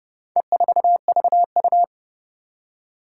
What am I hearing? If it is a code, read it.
Morse code, heard as a keyed tone, about 30 wpm, 700 Hz. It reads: E4VU
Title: Four Challenging Letter-Number Combinations 30wpm